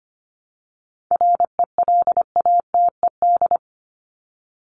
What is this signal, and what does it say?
Morse code, heard as a keyed tone, about 25 wpm, 700 Hz. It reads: RELATED
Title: ED Suffix - Top 100 Words with Base Words 25wpm